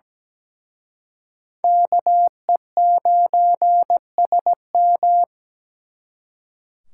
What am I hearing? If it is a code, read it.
Morse code, heard as a keyed tone, about 20 wpm, 700 Hz. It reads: KE9SM
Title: Speed Racing - USA 2x2 Call Signs 17wpm